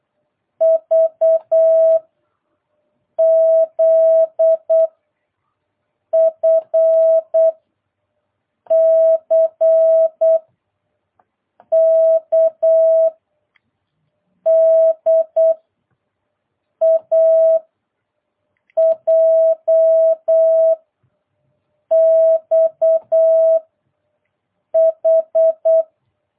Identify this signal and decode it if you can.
Morse code, heard as a keyed tone, about 8 wpm, 650 Hz. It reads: V Z F C K D A J X H